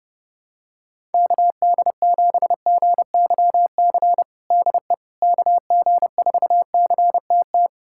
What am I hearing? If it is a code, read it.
Morse code, heard as a keyed tone, about 32 wpm, 700 Hz. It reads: KD7GYC DE KG4CTT
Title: USA Callsign DE Callsign - 6 Characters - 30wpm